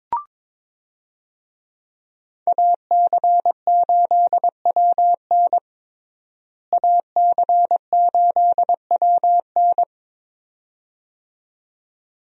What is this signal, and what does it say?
Morse code, heard as a keyed tone, about 22 wpm, 700 Hz. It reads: AC8WN AC8WN
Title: USA Call Signs - 5 Characters - Repeated 2x - 22wpm